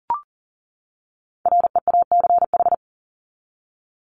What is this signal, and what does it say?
Morse code, heard as a keyed tone, about 40 wpm, 700 Hz. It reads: REACH
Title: ED Suffix - Top 100 Words with Base Words 40wpm